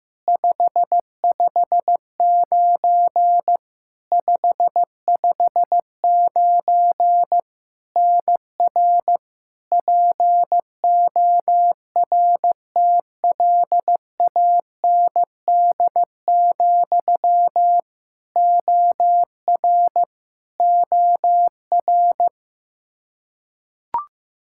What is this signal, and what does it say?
Morse code, heard as a keyed tone, about 15 wpm, 700 Hz. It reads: 559 559 NR PORTLAND, OR OR
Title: Practice QSO - Exchange 1 - Beginning - 15wpm